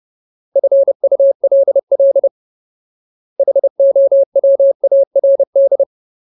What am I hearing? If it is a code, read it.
Morse code, heard as a keyed tone, about 30 wpm, 550 Hz. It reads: FULL HOWARD
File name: Ultimate Word List / 30WPM Ultimate